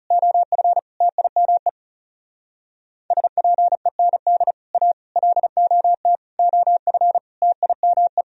Morse code, heard as a keyed tone, about 40 wpm, 700 Hz. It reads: OF TIME SPEND A LOT OF TIME